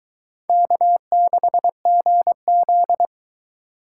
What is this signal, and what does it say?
Morse code, heard as a keyed tone, about 22 wpm, 700 Hz. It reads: K6GZ